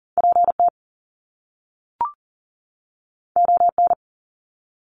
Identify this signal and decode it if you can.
Morse code, heard as a keyed tone, about 41 wpm, 700 Hz. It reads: PT ON
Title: Top Two Letter Combinations in Words 40wpm